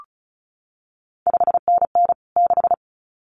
Morse code, heard as a keyed tone, about 36 wpm, 700 Hz. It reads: 5NN 6
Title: CQ WW DX Contest Exchange - 35wpm